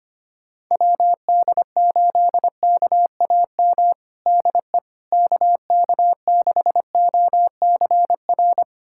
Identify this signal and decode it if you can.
Morse code, heard as a keyed tone, about 25 wpm, 700 Hz. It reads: WD8KAM DE KK6OCR
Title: USA Callsign DE Callsign 25wpm